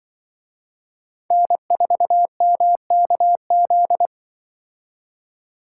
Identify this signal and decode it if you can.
Morse code, heard as a keyed tone, about 23 wpm, 700 Hz. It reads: N4MKZ
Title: Speed Racing - USA 1x3 Call Signs 20wpm